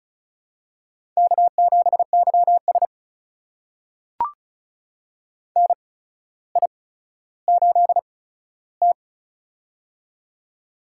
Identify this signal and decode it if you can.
Morse code, heard as a keyed tone, about 36 wpm, spaced as for 15 wpm, 700 Hz. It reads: K7YS NI8T
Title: USA Call Signs - 4 Characters - 8x Character Spacing - 35wpm